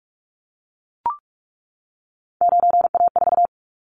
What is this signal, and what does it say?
Morse code, heard as a keyed tone, about 44 wpm, 700 Hz. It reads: 9A4
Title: Callsign Prefixes for Top 25 DXCC - 45wpm